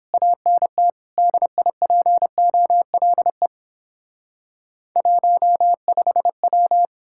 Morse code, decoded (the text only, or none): ANT DIPOLE 15W